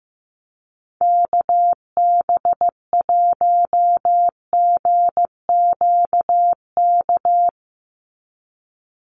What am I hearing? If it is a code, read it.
Morse code, heard as a keyed tone, about 16 wpm, 700 Hz. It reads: KB1GQK
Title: USA Call Signs 15wpm